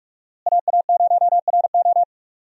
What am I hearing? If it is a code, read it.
Morse code, heard as a keyed tone, about 46 wpm, 700 Hz. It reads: AA0RO
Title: USA Call Signs - 5 Characters - Repeated 3x - 45wpm